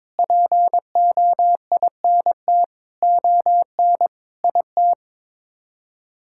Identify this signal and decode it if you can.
Morse code, heard as a keyed tone, about 22 wpm, 700 Hz. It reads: POINT ON IT
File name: Sets of 9 Words - 22wpm